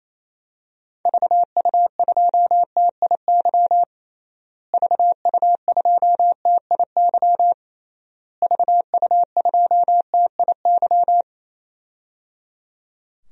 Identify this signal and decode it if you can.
Morse code, heard as a keyed tone, about 28 wpm, 700 Hz. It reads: VU2TIY VU2TIY VU2TIY
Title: DXCC-100 Call Signs - 6 Characters - Repeated 3x - 28wpm